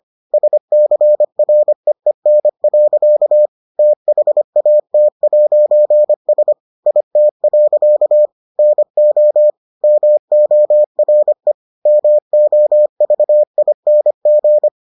Morse code, heard as a keyed tone, about 25 wpm, 600 Hz. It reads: SCREEN. THAT'S IT. NO MORE MOVING